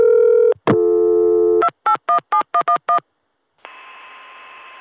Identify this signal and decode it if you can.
DTMF, heard as telephone touch tones, tones of about 100 ms. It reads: 2817111